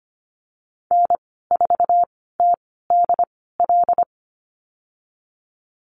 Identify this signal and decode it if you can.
Morse code, heard as a keyed tone, about 25 wpm, spaced as for 16 wpm, 700 Hz. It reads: N4TDL
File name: Lesson 15g - Number 5 - Call Signs 25wpm